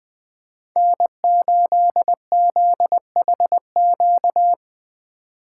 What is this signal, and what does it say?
Morse code, heard as a keyed tone, about 20 wpm, 700 Hz. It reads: N8ZHQ
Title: USA Call Signs - 5 Characters - 2x Character Spacing - 20wpm